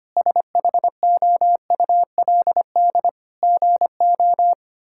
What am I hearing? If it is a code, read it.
Morse code, heard as a keyed tone, about 25 wpm, 700 Hz. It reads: SHOULD GO
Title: Sentences from Top 1600 Words 25wpm